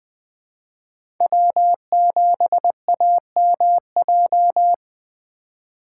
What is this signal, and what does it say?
Morse code, heard as a keyed tone, about 20 wpm, 700 Hz. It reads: W7AMJ
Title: USA Call Signs - 5 Characters - 4x Character Spacing - 20wpm